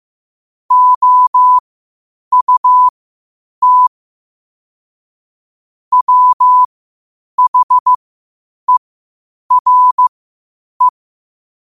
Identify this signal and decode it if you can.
Morse code, heard as a keyed tone, about 15 wpm, spaced as for 8 wpm, 1000 Hz. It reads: OUT WHERE
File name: qotdfiles / 8wpm